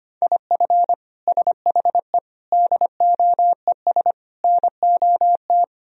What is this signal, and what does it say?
Morse code, heard as a keyed tone, about 25 wpm, 700 Hz. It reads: IF SHE DOES NOT
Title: Sentences from Top 1000 Words - Review Part 3 - 25wpm